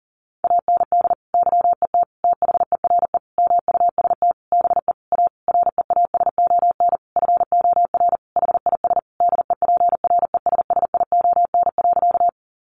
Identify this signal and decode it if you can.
Morse code, heard as a keyed tone, about 40 wpm, 700 Hz. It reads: AND YET THERE MUST BE A REASON FOR HIS DEPRESSION.
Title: Sentences from Top 1300 Words 40wpm